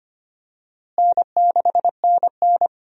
Morse code, heard as a keyed tone, about 25 wpm, 700 Hz. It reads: N6NN
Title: USA Call Signs - 4 Characters - Repeated 2x - 25wpm